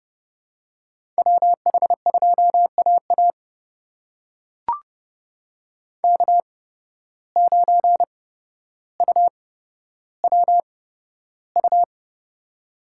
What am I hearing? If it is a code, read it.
Morse code, heard as a keyed tone, about 30 wpm, spaced as for 13 wpm, 700 Hz. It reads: WH2AA K9UWU
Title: USA Call Signs - 5 Characters - 8x Character Spacing - 30wpm